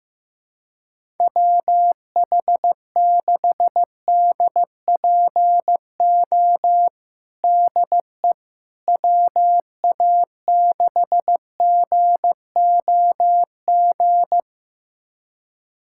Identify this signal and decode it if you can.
Morse code, heard as a keyed tone, about 15 wpm, 700 Hz. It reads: WH6DPO DE WA6GOG
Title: USA Callsign DE Callsign 15wpm